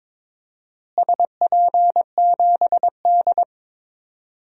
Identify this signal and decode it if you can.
Morse code, heard as a keyed tone, about 22 wpm, 700 Hz. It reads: SP7D
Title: DXCC-100 Call Signs - 4 Characters - Repeated 3x - 22wpm